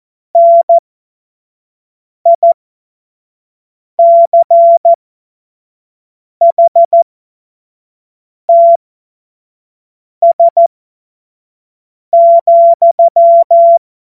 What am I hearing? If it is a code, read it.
Morse code, heard as a keyed tone, about 14 wpm, spaced as for 5 wpm, 680 Hz. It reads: NICHTS,
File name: Witz-12-August-2025-25-F70